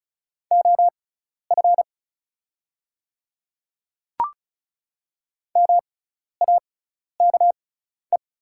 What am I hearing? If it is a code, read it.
Morse code, heard as a keyed tone, about 34 wpm, spaced as for 12 wpm, 690 Hz. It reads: OF MAKE
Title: 6x Character Spacing - Top 100 Words - 35wpm